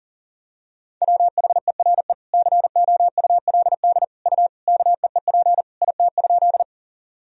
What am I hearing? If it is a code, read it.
Morse code, heard as a keyed tone, about 40 wpm, 700 Hz. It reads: WHERE COULD U KEEP IT?
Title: Sentences from Top 300 Words and Abbreviations 40wpm